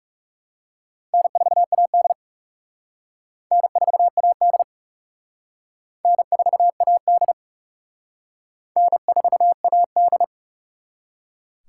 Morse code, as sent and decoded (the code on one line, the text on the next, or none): -. ....- .- -.. / -. ....- .- -.. / -. ....- .- -.. / -. ....- .- -..
N4AD N4AD N4AD N4AD